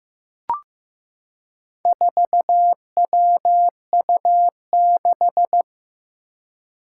Morse code, heard as a keyed tone, about 15 wpm, 700 Hz. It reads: 4WU6
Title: Four Challenging Letter-Number Combinations 15wpm